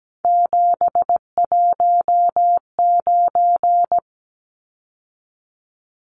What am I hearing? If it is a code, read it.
Morse code, heard as a keyed tone, about 17 wpm, 700 Hz. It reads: 719
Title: Numbers - 3-Digit 17wpm